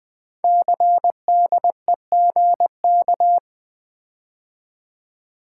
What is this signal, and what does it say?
Morse code, heard as a keyed tone, about 20 wpm, 700 Hz. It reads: CDEGK
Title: Code Groups - 5 Letters - 20wpm